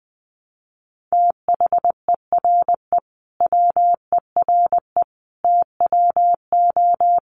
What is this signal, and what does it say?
Morse code, heard as a keyed tone, about 19 wpm, 700 Hz. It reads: THERE WERE TWO
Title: Sentences from Top 100 Words 20wpm